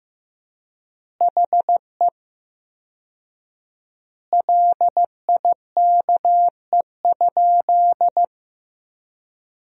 Morse code, heard as a keyed tone, about 15 wpm, 700 Hz. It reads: HE LIKE?
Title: 4x Word-Spacing - Sentences from Top 100 Words - 15wpm